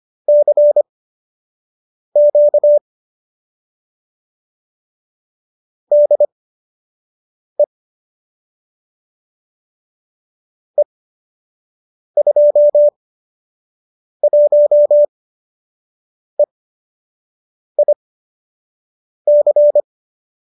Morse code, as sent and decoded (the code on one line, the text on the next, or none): -.-. --.- / -.. / . ..--- .---- . .. -.-.
CQ DE E21EIC